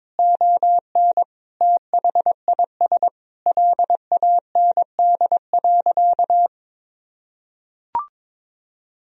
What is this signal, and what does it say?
Morse code, heard as a keyed tone, about 22 wpm, 700 Hz. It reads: ON THIS LAND.
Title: Sentences from Top 1000 Words - Review Part 4 - 22wpm